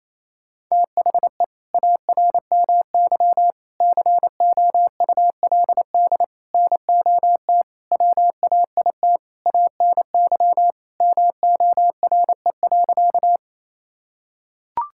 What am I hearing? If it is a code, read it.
Morse code, heard as a keyed tone, about 27 wpm, 700 Hz. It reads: THE ARMY COULD NOT WAIT ANY MORE.
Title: Sentences from Top 1000 Words - Review Part 5 - 28wpm